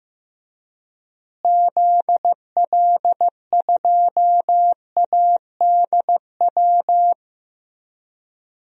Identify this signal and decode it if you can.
Morse code, heard as a keyed tone, about 15 wpm, 700 Hz. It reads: ZL2ADW
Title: DXCC-100 Call Signs - 6 Characters - Repeated 3x - 15wpm